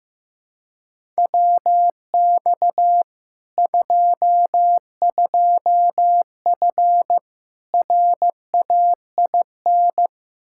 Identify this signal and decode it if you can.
Morse code, heard as a keyed tone, about 15 wpm, 700 Hz. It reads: WX 22F RAIN
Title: Practice QSO - Exchange 2 - Info - 15wpm